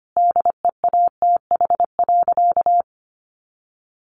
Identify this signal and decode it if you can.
Morse code, heard as a keyed tone, about 25 wpm, 700 Hz. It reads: DEATH.
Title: Sentences from Top 1100 Words 25wpm